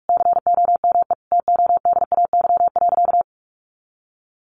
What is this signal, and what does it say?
Morse code, heard as a keyed tone, about 44 wpm, 700 Hz. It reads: COME TODAY.
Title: Additional Easy Phrases 45wpm